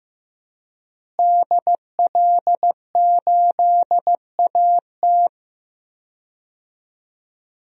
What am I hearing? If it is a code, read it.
Morse code, heard as a keyed tone, about 15 wpm, 700 Hz. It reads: DL8AT
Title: DXCC-100 Call Signs - 5 Characters - Repeated 3x - 15wpm